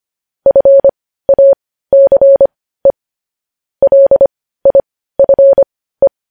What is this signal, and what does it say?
Morse code, heard as a keyed tone, about 25 wpm, spaced as for 15 wpm, 550 Hz. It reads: FACE LIFE
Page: news-headlines-in-morse-code-at-15-wpm